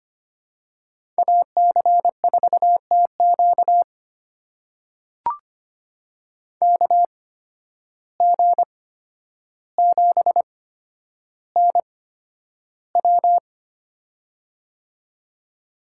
Morse code, heard as a keyed tone, about 25 wpm, spaced as for 11 wpm, 700 Hz. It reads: AC4TQ KG7NW
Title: USA Call Signs - 5 Characters - 8x Character Spacing - 25wpm